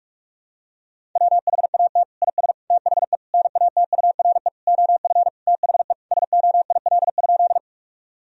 Morse code, heard as a keyed tone, about 46 wpm, 700 Hz. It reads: WHAT IS THE NATURE OF THE SOIL?
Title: Sentences from Top 1000 Words - Review Part 4 - 45wpm